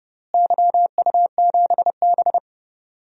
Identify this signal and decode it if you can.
Morse code, heard as a keyed tone, about 30 wpm, 700 Hz. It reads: YU7B
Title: DXCC-100 Call Signs - 4 Characters - Repeated 2x - 30wpm